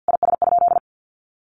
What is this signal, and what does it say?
Morse code, heard as a keyed tone, about 50 wpm, 700 Hz. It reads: IS?
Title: Sentences from Top 1500 Words 50wpm